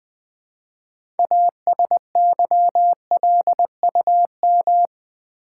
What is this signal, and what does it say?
Morse code, heard as a keyed tone, about 20 wpm, 700 Hz. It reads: ASYLUM
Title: 6 Letter Words - First 3 Letters Given - 20wpm